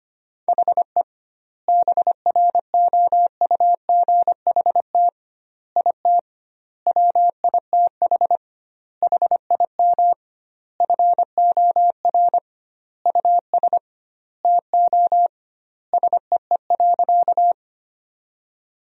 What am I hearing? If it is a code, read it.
Morse code, heard as a keyed tone, about 25 wpm, 700 Hz. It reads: HE BROUGHT IT WITH HIM FOR US TO SEE.
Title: 2x Word-Spacing - Sentences from Top 500 Words - 25wpm